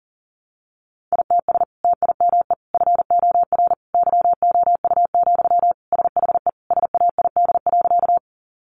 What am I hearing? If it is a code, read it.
Morse code, heard as a keyed tone, about 37 wpm, 700 Hz. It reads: ITS TIME FOR YOU, SHE SAID.
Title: Sentences from Top 600 Words 40wpm